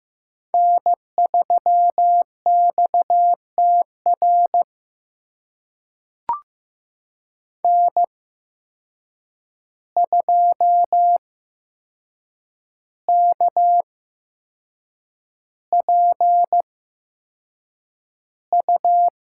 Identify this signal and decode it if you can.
Morse code, heard as a keyed tone, about 15 wpm, spaced as for 6 wpm, 700 Hz. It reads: N3XTR N2KPU
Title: USA Call Signs - 5 Characters - 8x Character Spacing - 15wpm